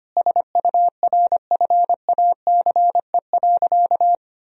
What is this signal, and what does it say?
Morse code, heard as a keyed tone, about 25 wpm, 700 Hz. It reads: SURFACE.
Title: Sentences from Top 1000 Words - Review Part 3 - 25wpm